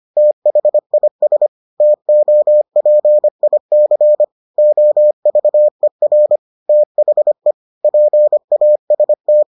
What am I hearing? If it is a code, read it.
Morse code, heard as a keyed tone, about 25 wpm, 600 Hz. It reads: THIS TOPIC OVER THE PAST